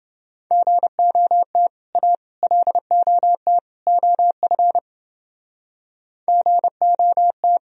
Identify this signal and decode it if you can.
Morse code, heard as a keyed tone, about 29 wpm, 700 Hz. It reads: GOT A LOT OF GOT